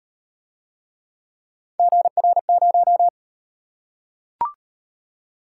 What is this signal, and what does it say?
Morse code, heard as a keyed tone, about 37 wpm, 700 Hz. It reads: GR0